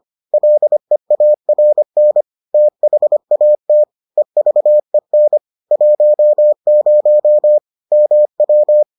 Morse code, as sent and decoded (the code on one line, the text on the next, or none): .-.. .- .-. -. / - .... .- - / . ...- . -. / .---- ----- / -- .--
LEARN THAT EVEN 10 MW